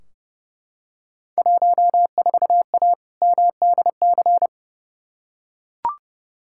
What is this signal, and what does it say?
Morse code, heard as a keyed tone, about 31 wpm, 700 Hz. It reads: 14A MDC
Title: Field Day Exchanges Encore - 30wpm